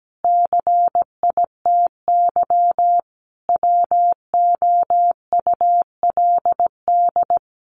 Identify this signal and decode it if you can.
Morse code, heard as a keyed tone, about 17 wpm, 700 Hz. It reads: CITY WOULD
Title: Sentences from Top 1100 Words 17wpm